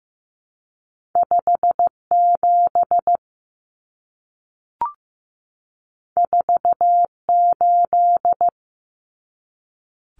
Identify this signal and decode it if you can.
Morse code, heard as a keyed tone, about 15 wpm, 700 Hz. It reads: 57 48
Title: Common Cut Numbers - 2-Digit 15wpm